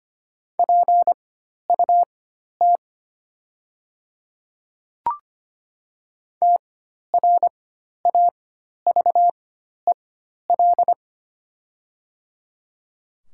Morse code, heard as a keyed tone, about 25 wpm, spaced as for 12 wpm, 700 Hz. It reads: PUT TRAVEL
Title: 4x Character Spacing - Top 500 Words - 25wpm